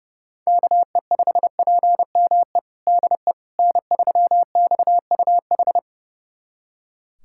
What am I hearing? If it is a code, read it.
Morse code, heard as a keyed tone, about 30 wpm, 700 Hz. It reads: KE5PME DE N3XUH
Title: USA Callsign DE Callsign 30wpm